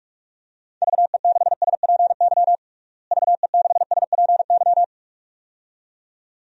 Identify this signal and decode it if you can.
Morse code, heard as a keyed tone, about 46 wpm, 700 Hz. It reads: VE6SPY VE6SPY
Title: DXCC-100 Call Signs - 6 Characters - Repeated 2x - 45wpm